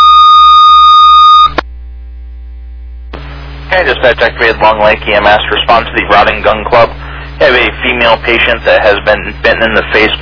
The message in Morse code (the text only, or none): none